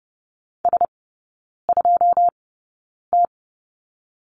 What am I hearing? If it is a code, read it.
Morse code, heard as a keyed tone, about 30 wpm, spaced as for 9 wpm, 690 Hz. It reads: S2T
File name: Lesson 17e - Letter u - Three Characters 30wpm